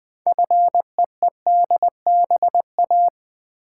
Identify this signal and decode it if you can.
Morse code, heard as a keyed tone, about 21 wpm, 700 Hz. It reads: FEEDBA
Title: Build A Word - 2 Letters At A Time - 8-Letter Words - 20wpm